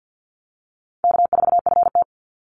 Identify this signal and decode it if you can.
Morse code, heard as a keyed tone, about 50 wpm, 700 Hz. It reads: K4FT
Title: Easy Call Signs - 50wpm